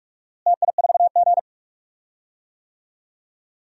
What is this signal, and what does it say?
Morse code, heard as a keyed tone, about 46 wpm, 700 Hz. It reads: TI4G
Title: DXCC-100 Call Signs - 4 Characters - Repeated 2x - 45wpm